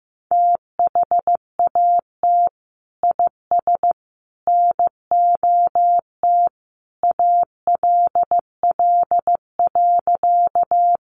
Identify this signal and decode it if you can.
Morse code, heard as a keyed tone, about 15 wpm, 700 Hz. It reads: THAT IS NOT ALL.
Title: Sentences from Top 600 Words 15wpm